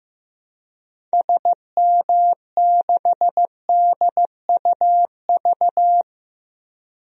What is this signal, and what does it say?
Morse code, heard as a keyed tone, about 15 wpm, 700 Hz. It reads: SM6DUV
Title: DXCC-100 Call Signs - 6 Characters - 15wpm